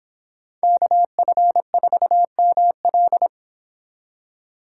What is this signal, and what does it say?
Morse code, heard as a keyed tone, about 29 wpm, 700 Hz. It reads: KF4ML